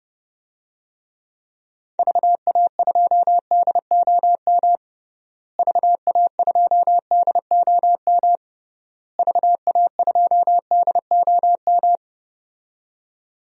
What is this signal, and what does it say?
Morse code, heard as a keyed tone, about 30 wpm, 700 Hz. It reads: VA2DOM VA2DOM VA2DOM
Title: DXCC-100 Call Signs - 6 Characters - Repeated 3x - 30wpm